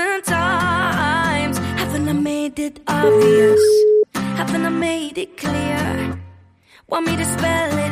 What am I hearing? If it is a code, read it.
Ringback tone (Europe, 425 Hz), with speech.